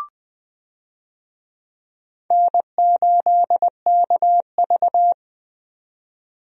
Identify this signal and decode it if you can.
Morse code, heard as a keyed tone, about 20 wpm, 700 Hz. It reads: N8KV
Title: USA Call Signs - 4 Characters - 20wpm